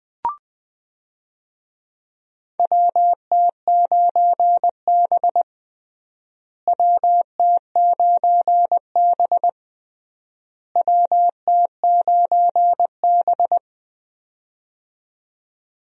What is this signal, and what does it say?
Morse code, heard as a keyed tone, about 20 wpm, 700 Hz. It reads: WT9B WT9B WT9B
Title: USA Call Signs - 4 Characters - Repeated 3x - 20wpm